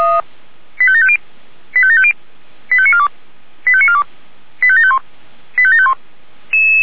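Five-tone selective call (ZVEI), heard as quick runs of five tones, tones of about 70 ms. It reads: 85750 85750 85942 85942 85751 85751
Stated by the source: ZVEI five-tone selective call received by SDR.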